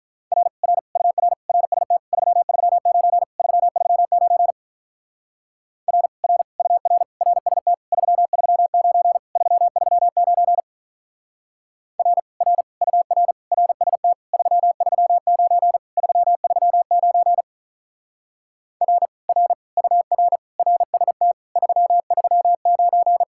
Morse code, heard as a keyed tone, about 43 wpm, 700 Hz. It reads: R R UR RST 339 339 R R UR RST 339 339 R R UR RST 339 339 R R UR RST 339
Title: Speed-Racing Common QSO Phrases - 35wpm